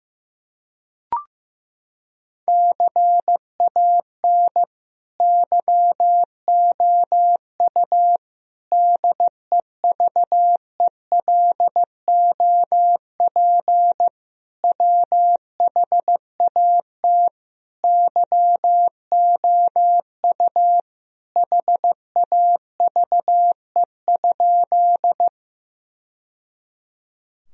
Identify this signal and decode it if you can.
Morse code, heard as a keyed tone, about 15 wpm, 700 Hz. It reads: CAN YOU DEVELOP WHAT YOU HAVE?
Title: Sentences from Top 1000 Words - Review Part 2 - 15wpm